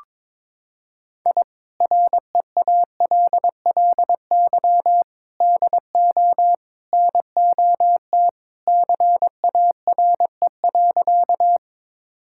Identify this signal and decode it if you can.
Morse code, heard as a keyed tone, about 22 wpm, 700 Hz. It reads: I REALLY DO NOT CARE.
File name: practice / 22wpm / Sentences with 4-Word Phrases - 22wpm